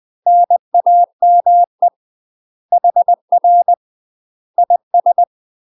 Morse code, heard as a keyed tone, about 20 wpm, 700 Hz. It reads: NAME HR IS
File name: UK MASTER QSOs TRAINING / 20WPM / UK MASTER QSOs 0011